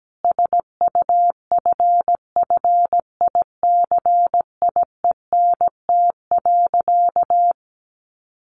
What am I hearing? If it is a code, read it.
Morse code, heard as a keyed tone, about 17 wpm, 700 Hz. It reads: SUFFICIENT.
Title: Sentences from Top 1400 Words 17wpm